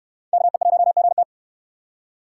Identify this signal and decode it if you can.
Morse code, heard as a keyed tone, about 69 wpm, 700 Hz. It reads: KE1GT